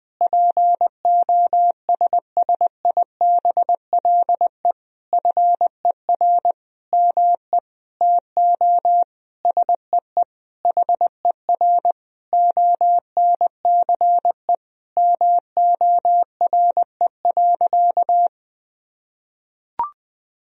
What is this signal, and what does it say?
Morse code, heard as a keyed tone, about 21 wpm, 700 Hz. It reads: POSSIBLE FER ME TO SEE HER ONCE MORE.
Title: Sentences from Top 500 Words and Abbreviations 20wpm